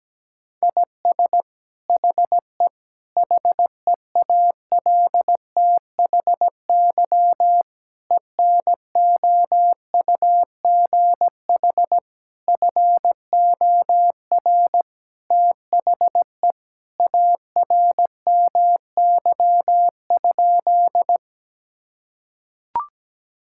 Morse code, heard as a keyed tone, about 17 wpm, 700 Hz. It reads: IS HE HEALTHY ENOUGH FOR THE ARMY?
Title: Sentences from Top 1000 Words - Review Part 5 - 17wpm